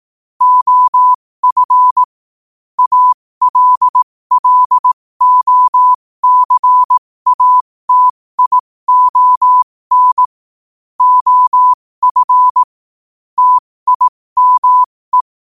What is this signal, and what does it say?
Morse code, heard as a keyed tone, about 18 wpm, spaced as for 15 wpm, 1000 Hz. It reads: OF ALLOCATION OF TIME
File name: qotdfiles / 15wpm